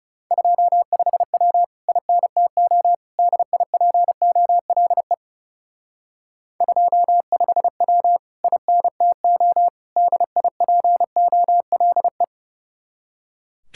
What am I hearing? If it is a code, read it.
Morse code, heard as a keyed tone, about 37 wpm, 700 Hz. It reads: 25W INTO DIPOLE 25W INTO DIPOLE
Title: Speed-Racing Common QSO Phrases - 30wpm